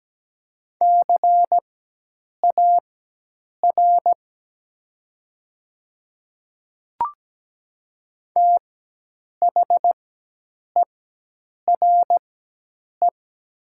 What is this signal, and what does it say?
Morse code, heard as a keyed tone, about 17 wpm, spaced as for 8 wpm, 700 Hz. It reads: CAR THERE
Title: 4x Character Spacing - Top 500 Words - 17wpm